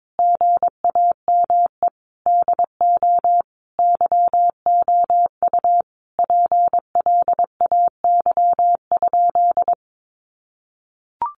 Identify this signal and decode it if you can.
Morse code, heard as a keyed tone, about 22 wpm, 700 Hz. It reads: GAME DO YOU PLAY?
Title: Sentences from Top 1400 Words 22wpm